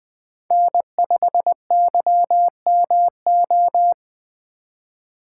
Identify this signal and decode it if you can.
Morse code, heard as a keyed tone, about 21 wpm, 700 Hz. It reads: N5YMO